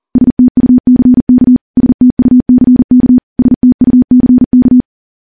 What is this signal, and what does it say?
Morse code, heard as a keyed tone, about 40 wpm, 250 Hz. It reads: STUCK STUCK STUCK
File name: morse / morse-training-40WPM